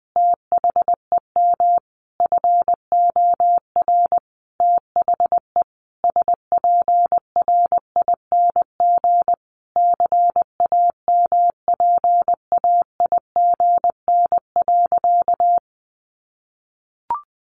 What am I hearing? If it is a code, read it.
Morse code, heard as a keyed tone, about 20 wpm, 700 Hz. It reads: THEM FOR THE SPRING CAMPAIGN.
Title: Sentences from Top 1200 Words 20wpm